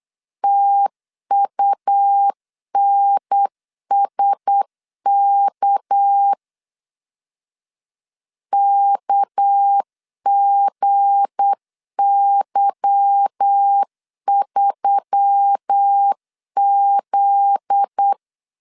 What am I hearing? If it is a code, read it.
Morse code, heard as a keyed tone, about 8 wpm, 800 Hz. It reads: TUNSK KGY3Z
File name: LEZIONI / 8 YJ45 / 8 Y 8wpm 3245